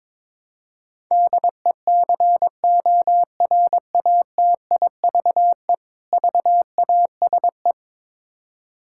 Morse code, heard as a keyed tone, about 21 wpm, 700 Hz. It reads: DECORATIVE VASE
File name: Difficult Words with Hints - 22wpm